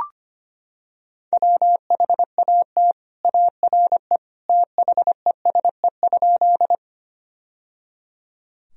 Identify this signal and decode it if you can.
Morse code, heard as a keyed tone, about 25 wpm, 700 Hz. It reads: WHAT ARE THESE?
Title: Sentences from Top 1000 Words - Review Part 1 - 25wpm